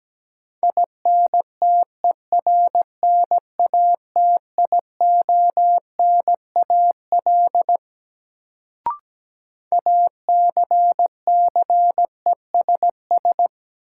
Morse code, heard as a keyed tone, about 17 wpm, 700 Hz. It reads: INTERNATIONAL ACCESS
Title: Sentences from Top 1000 Words - Review Part 4 - New Word Drill - 17wpm